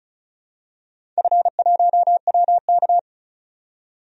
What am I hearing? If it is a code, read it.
Morse code, heard as a keyed tone, about 36 wpm, 700 Hz. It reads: F1WK
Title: DXCC-100 Call Signs - 4 Characters - 35wpm